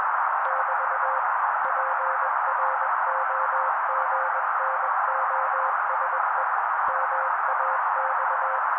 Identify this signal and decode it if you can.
Morse code, heard as a keyed tone, about 21 wpm, 530 Hz. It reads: <BT> PROGNOSE MAX
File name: daily / 20wpm noise